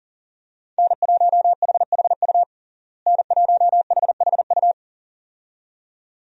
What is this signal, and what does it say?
Morse code, heard as a keyed tone, about 40 wpm, 700 Hz. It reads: N1HHU N1HHU
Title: USA Call Signs - 5 Characters - Repeated 2x - 40wpm